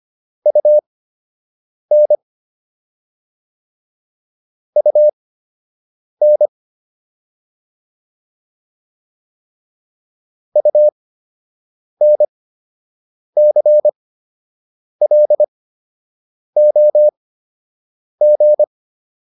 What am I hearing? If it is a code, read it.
Morse code, heard as a keyed tone, about 25 wpm, spaced as for 7 wpm, 600 Hz. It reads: UN UN UNCLOG